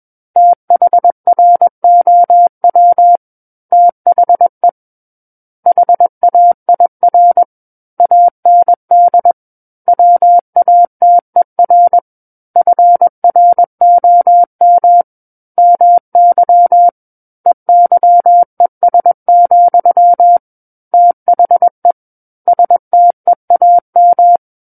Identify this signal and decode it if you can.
Morse code, heard as a keyed tone, about 21 wpm, 700 Hz. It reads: THROW THE HAIR AND WATER FROM MY EYES, THE STEAM
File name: Book One / 12-Chapter 12 - 21 WPM